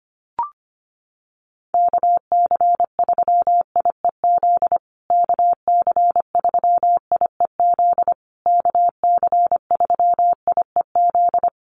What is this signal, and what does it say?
Morse code, heard as a keyed tone, about 25 wpm, 700 Hz. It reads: KC3IEZ KC3IEZ KC3IEZ
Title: USA Call Signs Repeated 3x 25wpm